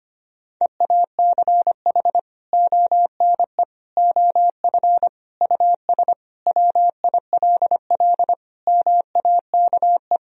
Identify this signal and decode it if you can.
Morse code, heard as a keyed tone, about 25 wpm, 700 Hz. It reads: EACH ONE OF US WILL MAKE
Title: Sentences from Top 1000 Words - Review Part 2 - 25wpm